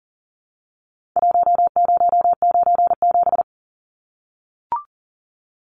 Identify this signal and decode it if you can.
Morse code, heard as a keyed tone, about 40 wpm, 700 Hz. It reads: 1097
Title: Numbers - Contest Style - 4-Digit 40wpm